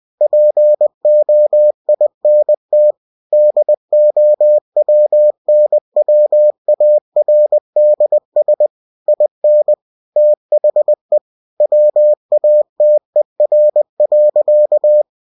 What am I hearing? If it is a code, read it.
Morse code, heard as a keyed tone, about 20 wpm, 600 Hz. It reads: POINT DOWNWARDS IN THE WATER.
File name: cw-books / winnie-the-pooh / winnie-20wpm-page-0100